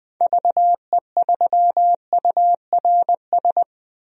Morse code, heard as a keyed tone, about 20 wpm, 700 Hz. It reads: VE3URS
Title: DXCC-100 Call Signs - 6 Characters - 20wpm